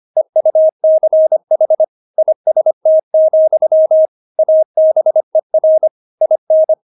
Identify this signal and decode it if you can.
Morse code, heard as a keyed tone, about 25 wpm, 630 Hz. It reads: EUCH IST, ABER IN
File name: Witz6 25wpm